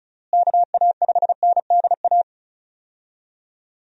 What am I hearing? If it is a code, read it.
Morse code, heard as a keyed tone, about 36 wpm, 700 Hz. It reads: KA5NDA
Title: USA Call Signs - 6 Characters - Repeated 3x - 35wpm